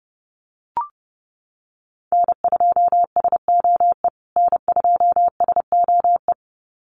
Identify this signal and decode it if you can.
Morse code, heard as a keyed tone, about 30 wpm, 700 Hz. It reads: N2SOE N2SOE
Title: USA Call Signs Repeated 2x 30wpm